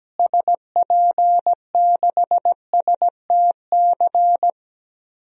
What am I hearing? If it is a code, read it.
Morse code, heard as a keyed tone, about 17 wpm, 700 Hz. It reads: SP6STC
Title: DXCC-100 Call Signs - 6 Characters - Repeated 3x - 17wpm